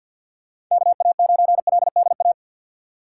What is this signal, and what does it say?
Morse code, heard as a keyed tone, about 51 wpm, 700 Hz. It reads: KA9LDA
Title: USA Call Signs - 6 Characters - 4x Character Spacing - 50wpm